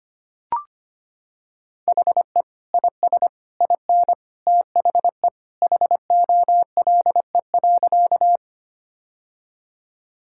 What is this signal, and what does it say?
Morse code, heard as a keyed tone, about 25 wpm, 700 Hz. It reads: HE IS IN THE HOLE.
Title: Sentences from Top 1600 Words 25wpm